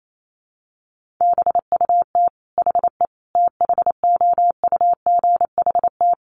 Morse code, heard as a keyed tone, about 28 wpm, 700 Hz. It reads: BUT HE THOUGHT
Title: Sentences from Top 1400 Words 28wpm